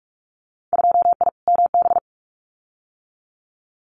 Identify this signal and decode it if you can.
Morse code, heard as a keyed tone, about 46 wpm, 700 Hz. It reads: 2I MB
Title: Winter Field Day Exchanges Encore 45wpm